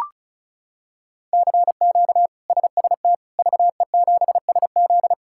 Morse code, heard as a keyed tone, about 36 wpm, 700 Hz. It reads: CQ SST VE7SZ
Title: K1USN SST QSO Practice - 35wpm